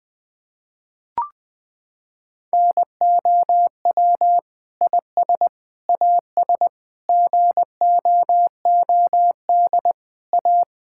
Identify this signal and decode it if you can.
Morse code, heard as a keyed tone, about 20 wpm, 700 Hz. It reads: NOW IS AS GOOD A
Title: Sentences with 8-Word Phrases - 20wpm